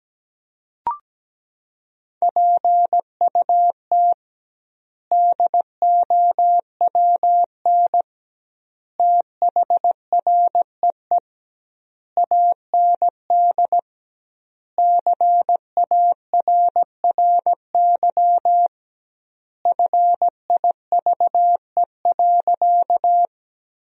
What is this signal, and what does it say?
Morse code, heard as a keyed tone, about 17 wpm, 700 Hz. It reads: PUT DOWN THREE AND CARRY FIVE.
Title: 2x Word-Spacing - Sentences from Top 400 Words - 17wpm